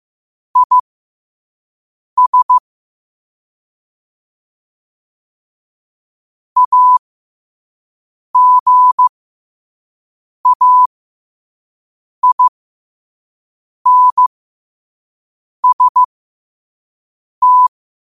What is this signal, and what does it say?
Morse code, heard as a keyed tone, about 15 wpm, spaced as for 5 wpm, 1000 Hz. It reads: IS AGAINST
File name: qotdfiles / 5wpm